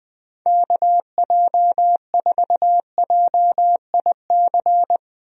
Morse code, heard as a keyed tone, about 20 wpm, 700 Hz. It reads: KJ4JIC
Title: USA Call Signs - 6 Characters - 20wpm